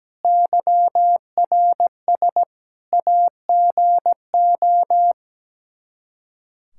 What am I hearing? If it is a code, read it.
Morse code, heard as a keyed tone, about 17 wpm, 700 Hz. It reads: YRS AGO
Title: Practice QSO - Exchange 3 - More Info - 17wpm